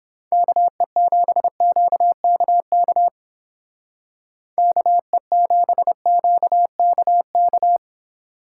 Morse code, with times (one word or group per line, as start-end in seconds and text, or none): 0.3-3.1 KE7QKK
4.6-7.8 KE7QKK